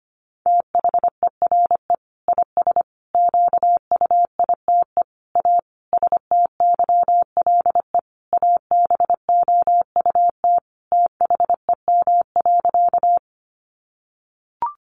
Morse code, heard as a keyed tone, about 25 wpm, 700 Hz. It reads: THERE IS QUITE A STYLE ABOUT THEM.
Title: Sentences from Top 800 Words 25wpm